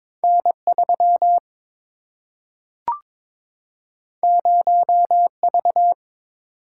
Morse code, heard as a keyed tone, about 22 wpm, 700 Hz. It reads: N3 0V
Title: Single-Letter Single-Number 22wpm